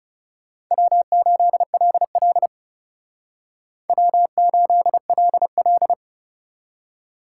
Morse code, heard as a keyed tone, about 37 wpm, 700 Hz. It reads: W8LL W8LL
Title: Speed Racing - USA 1x2 Call Signs 30wpm